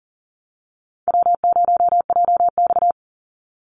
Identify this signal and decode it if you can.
Morse code, heard as a keyed tone, about 40 wpm, 700 Hz. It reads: W0JX
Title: Easy Call Signs Repeated 3x - 40wpm